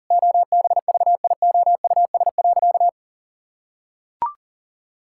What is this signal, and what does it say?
Morse code, heard as a keyed tone, about 40 wpm, 700 Hz. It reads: OBVIOUS.
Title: Sentences from Top 1600 Words 40wpm